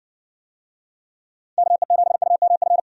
Morse code, heard as a keyed tone, about 57 wpm, 700 Hz. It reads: KE7UMF